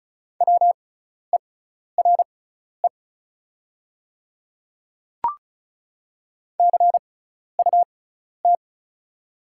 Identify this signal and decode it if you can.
Morse code, heard as a keyed tone, about 36 wpm, spaced as for 12 wpm, 700 Hz. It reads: WERE CUT
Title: 6x Character Spacing - Top 500 Words - 35wpm